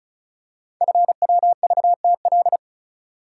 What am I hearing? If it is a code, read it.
Morse code, heard as a keyed tone, about 36 wpm, 700 Hz. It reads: FWVTL
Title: Code Groups - 5 Letter:Numbers - 35wpm